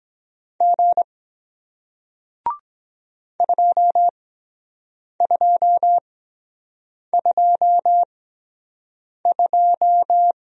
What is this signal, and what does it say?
Morse code, heard as a keyed tone, about 23 wpm, spaced as for 7 wpm, 700 Hz. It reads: G 2222